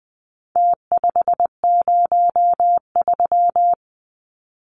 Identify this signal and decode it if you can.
Morse code, heard as a keyed tone, about 20 wpm, 700 Hz. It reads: T503
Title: Numbers - Contest Style - 4-Digit 20wpm